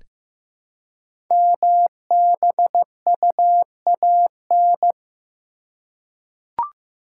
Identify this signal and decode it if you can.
Morse code, heard as a keyed tone, about 15 wpm, 700 Hz. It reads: MBUAN
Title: Code Groups - 5 Letters - 15wpm